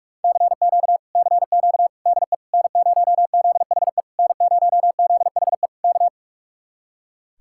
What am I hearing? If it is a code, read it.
Morse code, heard as a keyed tone, about 55 wpm, 700 Hz. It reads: CQ CQ DE N0ZHE N0ZHE K
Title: Speed-Racing Common QSO Phrases - 45wpm